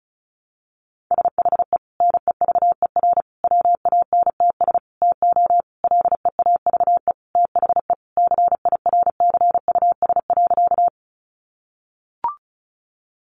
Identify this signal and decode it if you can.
Morse code, heard as a keyed tone, about 34 wpm, 700 Hz. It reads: SHE NEVER WANTS TO LEAVE THE CIRCUS.
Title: Easy Phrases 35wpm